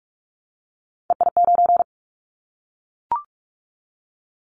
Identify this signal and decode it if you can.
Morse code, heard as a keyed tone, about 44 wpm, 700 Hz. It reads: EI9
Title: Callsign Prefixes for Top 50 DXCC - 45wpm